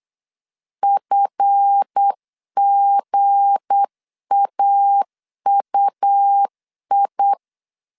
Morse code, heard as a keyed tone, about 8 wpm, 800 Hz. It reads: FGAUI
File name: LEZIONI / 7 GF9Z / 7 GF 8wpm 3108